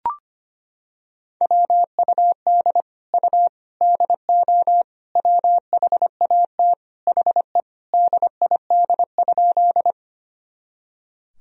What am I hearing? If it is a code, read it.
Morse code, heard as a keyed tone, about 25 wpm, 700 Hz. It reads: WUD U DO WHAT HE DID?
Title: Sentences from Top 100 Words and Abbreviations 25wpm